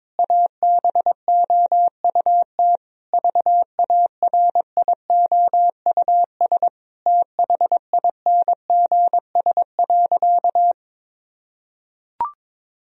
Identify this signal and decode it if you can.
Morse code, heard as a keyed tone, about 22 wpm, 700 Hz. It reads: ABOUT VARIOUS THINGS.
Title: Sentences from Top 1000 Words - Review Part 4 - 22wpm